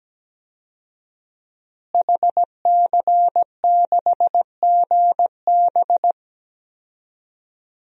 Morse code, heard as a keyed tone, about 17 wpm, 700 Hz. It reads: HC6GB
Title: DXCC-100 Call Signs - 5 Characters - Repeated 2x - 17wpm